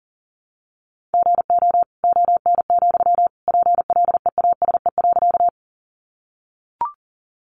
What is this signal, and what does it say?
Morse code, heard as a keyed tone, about 40 wpm, 700 Hz. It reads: GO ON, PLEASE.
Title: Sentences from Top 1500 Words 40wpm